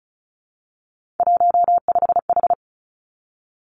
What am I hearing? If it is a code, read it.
Morse code, heard as a keyed tone, about 36 wpm, 700 Hz. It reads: 15H
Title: Three Challenging Letter-Number Combinations 35wpm